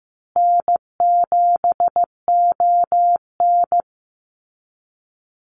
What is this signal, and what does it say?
Morse code, heard as a keyed tone, about 14 wpm, 700 Hz. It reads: N7ON
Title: Call Signs Repeated 3x - 15wpm